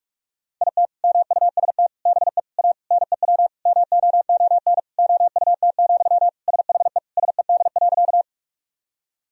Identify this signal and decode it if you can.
Morse code, heard as a keyed tone, about 41 wpm, 700 Hz. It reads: IT MUST BE A NEW MOON OUT, SHE SED.
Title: Sentences from Top 500 Words and Abbreviations 45wpm